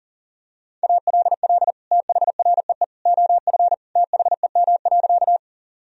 Morse code, heard as a keyed tone, about 39 wpm, 700 Hz. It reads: ALL THREE OF THEM.
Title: Sentences from Top 1000 Words - Review Part 1 - 40wpm